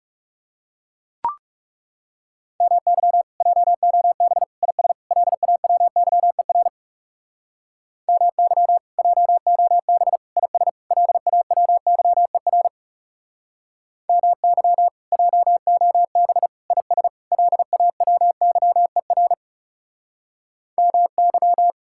Morse code, heard as a keyed tone, about 38 wpm, 700 Hz. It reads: MY JOB IS LAWYER MY JOB IS LAWYER MY JOB IS LAWYER MY